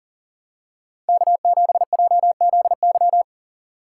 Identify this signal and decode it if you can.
Morse code, heard as a keyed tone, about 43 wpm, 690 Hz. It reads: K7JZY
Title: USA Call Signs - 5 Characters - 6x Character Spacing - 40wpm